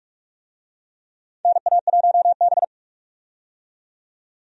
Morse code, heard as a keyed tone, about 46 wpm, 700 Hz. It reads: NA1B
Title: USA Call Signs - 4 Characters - 45wpm